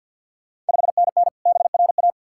Morse code, heard as a keyed tone, about 48 wpm, 700 Hz. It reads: 5NN BRA
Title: Ontario QSO Party Exchange 50wpm